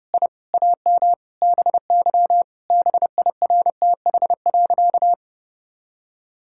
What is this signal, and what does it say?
Morse code, heard as a keyed tone, about 30 wpm, 700 Hz. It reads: I AM BY BIRTH.
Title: Sentences from Top 1000 Words - Review Part 5 - 30wpm